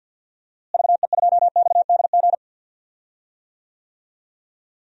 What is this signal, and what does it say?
Morse code, heard as a keyed tone, about 48 wpm, 700 Hz. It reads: VE2XDG
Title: DXCC-100 Call Signs - 6 Characters - Repeated 3x - 50wpm